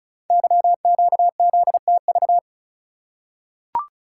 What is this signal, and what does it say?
Morse code, heard as a keyed tone, about 36 wpm, 690 Hz. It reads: YQZTV
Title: Code Groups - 5 Letter:Numbers - 35wpm